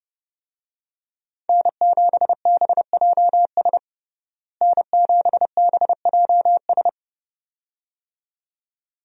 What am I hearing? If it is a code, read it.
Morse code, heard as a keyed tone, about 32 wpm, 700 Hz. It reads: N7BJS N7BJS
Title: USA Call Signs - 5 Characters - Repeated 2x - 30wpm